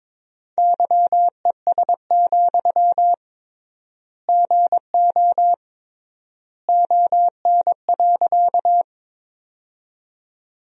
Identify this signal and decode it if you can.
Morse code, heard as a keyed tone, about 21 wpm, 700 Hz. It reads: YES, GO ON.
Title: 3x Word-Spacing - Sentences from Top 500 Words - 22wpm